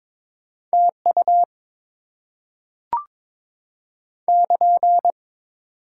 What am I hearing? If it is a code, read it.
Morse code, heard as a keyed tone, about 22 wpm, 700 Hz. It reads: TU <KN>
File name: Top 90 QSO Elements - 8x Character Spacing - 22wpm